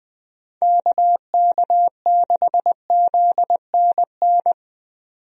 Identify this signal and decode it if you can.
Morse code, heard as a keyed tone, about 21 wpm, 700 Hz. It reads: KK6ZNN
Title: USA Call Signs - 6 Characters - 20wpm